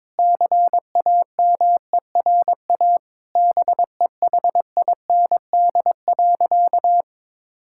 Morse code, heard as a keyed tone, about 22 wpm, 700 Hz. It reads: CAMERA BEHIND.
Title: Sentences from Top 1000 Words - Review Part 5 - 22wpm